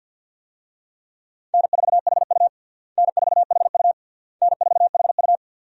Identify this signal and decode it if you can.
Morse code, heard as a keyed tone, about 50 wpm, 700 Hz. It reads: N4HU N4HU N4HU
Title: USA Call Signs - 4 Characters - Repeated 3x - 50wpm